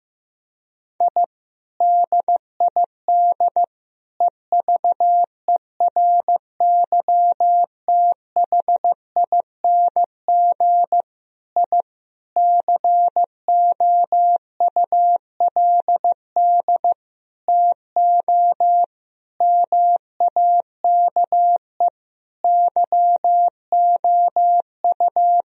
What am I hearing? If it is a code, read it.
Morse code, heard as a keyed tone, about 15 wpm, 700 Hz. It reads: I DID EVERYTHING I COULD TO MAKE YOU
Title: Sentences from Top 1000 Words - Review Part 1 - 15wpm